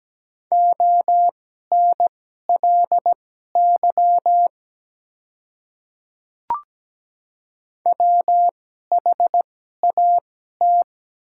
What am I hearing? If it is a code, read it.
Morse code, heard as a keyed tone, about 17 wpm, spaced as for 12 wpm, 700 Hz. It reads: ONLY WHAT
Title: 2x Character Spacing - Top 300 Words - 17wpm